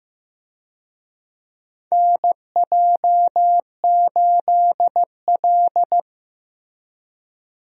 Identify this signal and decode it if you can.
Morse code, heard as a keyed tone, about 16 wpm, 700 Hz. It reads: NJ8L
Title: USA Call Signs - 4 Characters - Repeated 3x - 15wpm